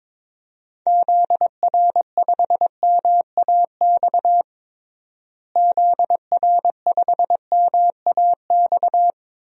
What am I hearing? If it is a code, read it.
Morse code, heard as a keyed tone, about 22 wpm, 700 Hz. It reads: ZR5MAX ZR5MAX
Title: DXCC-100 Call Signs - 6 Characters - Repeated 3x - 22wpm